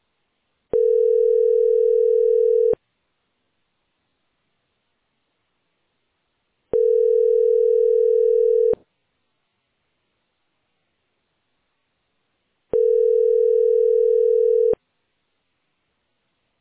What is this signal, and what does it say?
Ringback tone (North America), 440 + 480 Hz.